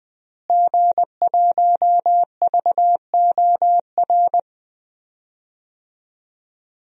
Morse code, heard as a keyed tone, about 20 wpm, 700 Hz. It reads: G1VOR
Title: DXCC-100 Call Signs - 5 Characters - Repeated 2x - 20wpm